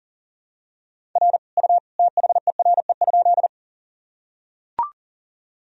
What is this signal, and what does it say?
Morse code, heard as a keyed tone, about 40 wpm, 700 Hz. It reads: R U THERE?
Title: Sentences from Top 100 Words and Abbreviations 40wpm